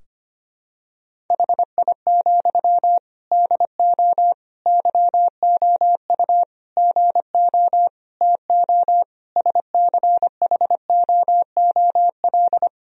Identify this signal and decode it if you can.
Morse code, heard as a keyed tone, about 25 wpm, 700 Hz. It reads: HI, DO YOU GO TO SCHOOL
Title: Sentences from Top 1000 Words - Review Part 3 - 25wpm